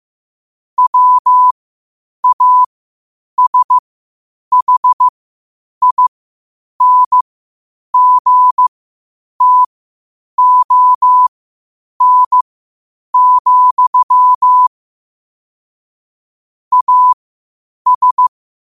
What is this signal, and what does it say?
Morse code, heard as a keyed tone, about 15 wpm, spaced as for 8 wpm, 1000 Hz. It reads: WASHINGTON, AS
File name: qotdfiles / 8wpm